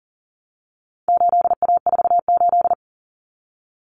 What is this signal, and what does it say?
Morse code, heard as a keyed tone, about 40 wpm, 700 Hz. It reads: 8A48
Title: Common Cut Numbers - 4-Digit 40wpm